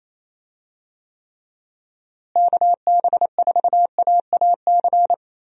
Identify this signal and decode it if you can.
Morse code, heard as a keyed tone, about 27 wpm, 700 Hz. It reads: KB4AAC
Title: USA Call Signs - 6 Characters - Repeated 2x - 28wpm